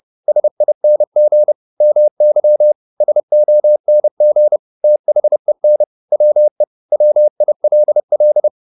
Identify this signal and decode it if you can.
Morse code, heard as a keyed tone, about 30 wpm, 600 Hz. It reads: SING MY SONG THEN WE WILL